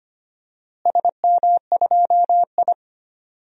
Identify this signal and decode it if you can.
Morse code, heard as a keyed tone, about 25 wpm, 700 Hz. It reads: SM2I
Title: DXCC-100 Call Signs - 4 Characters - Repeated 3x - 25wpm